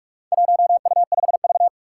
Morse code, heard as a keyed tone, about 44 wpm, 700 Hz. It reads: 1U5V
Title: Four Challenging Letter-Number Combinations 45wpm